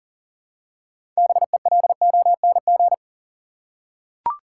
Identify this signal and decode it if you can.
Morse code, heard as a keyed tone, about 39 wpm, 700 Hz. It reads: BELONG
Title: Six Letter Words - 40wpm